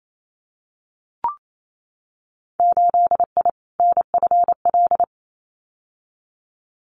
Morse code, heard as a keyed tone, about 28 wpm, 690 Hz. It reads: 8I NFL
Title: Winter Field Day Exchanges Encore 28wpm